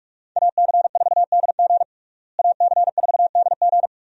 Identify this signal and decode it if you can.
Morse code, heard as a keyed tone, about 47 wpm, 700 Hz. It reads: AC4DG AC4DG
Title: USA Call Signs - 5 Characters - Repeated 3x - 45wpm